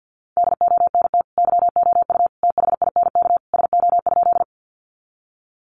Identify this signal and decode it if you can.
Morse code, heard as a keyed tone, about 50 wpm, 700 Hz. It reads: DONT YOU THINK SO?
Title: Sentences from Top 300 Words 50wpm